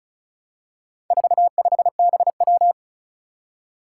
Morse code, heard as a keyed tone, about 36 wpm, 700 Hz. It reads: 45BW
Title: Four Challenging Letter-Number Combinations 35wpm